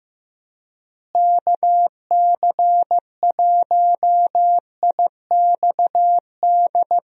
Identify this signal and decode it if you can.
Morse code, heard as a keyed tone, about 15 wpm, 700 Hz. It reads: KC1IXD
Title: USA Call Signs - 6 Characters - Repeated 3x - 15wpm